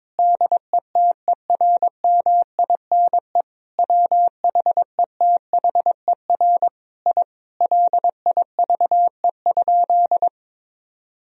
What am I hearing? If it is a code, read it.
Morse code, heard as a keyed tone, about 22 wpm, 700 Hz. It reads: DETERMINE WHETHER I LIVE?
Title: Sentences from Top 1000 Words - Review Part 4 - 22wpm